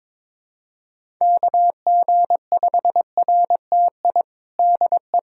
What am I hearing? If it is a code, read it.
Morse code, heard as a keyed tone, about 21 wpm, 700 Hz. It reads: KG5RTI DE
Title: USA Callsign DE Callsign 22wpm